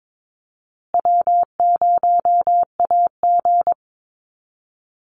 Morse code, heard as a keyed tone, about 22 wpm, 700 Hz. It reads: W0AG
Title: Call Signs Repeated 2x - 22wpm